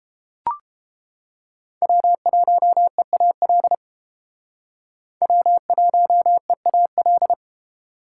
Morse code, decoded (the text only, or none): W1EAL W1EAL